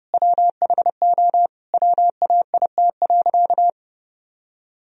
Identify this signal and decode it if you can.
Morse code, heard as a keyed tone, about 30 wpm, 700 Hz. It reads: WHO WAIT.
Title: Sentences from Top 1000 Words - Review Part 2 - 30wpm